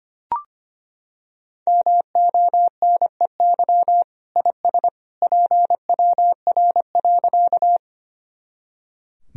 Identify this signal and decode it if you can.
Morse code, heard as a keyed tone, about 25 wpm, 700 Hz. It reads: MONEY IS PWR.
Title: Sentences from Top 500 Words and Abbreviations 25wpm